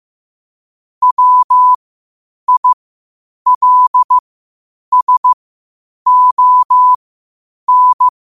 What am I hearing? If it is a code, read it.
Morse code, heard as a keyed tone, about 15 wpm, spaced as for 8 wpm, 1000 Hz. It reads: WILSON
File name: qotdfiles / 8wpm